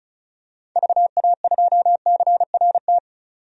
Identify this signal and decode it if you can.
Morse code, heard as a keyed tone, about 34 wpm, 700 Hz. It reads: VA2CRT
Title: DXCC-100 Call Signs - 6 Characters - 35wpm